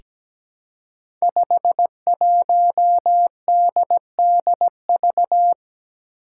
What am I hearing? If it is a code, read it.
Morse code, heard as a keyed tone, about 17 wpm, 700 Hz. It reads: 51DDV
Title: Five Challenging Letter-Number Combinations 17wpm